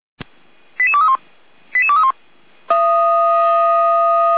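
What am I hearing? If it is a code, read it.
Five-tone selective call (ZVEI), heard as quick runs of five tones, tones of about 70 ms. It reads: 88231 88231
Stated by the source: ZVEI five-tone call 88102 received by SDR, feuerwehr-orlamuende.org fire brigade audio server.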